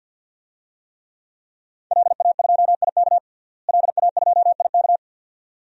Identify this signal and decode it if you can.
Morse code, heard as a keyed tone, about 46 wpm, 700 Hz. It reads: LA2IK LA2IK
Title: DXCC-100 Call Signs - 5 Characters - Repeated 2x - 50wpm